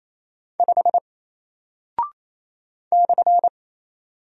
Morse code, heard as a keyed tone, about 27 wpm, 700 Hz. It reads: 5 /
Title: Single Character (Letter:Number:Punctuation:Prosign) Alt 28wpm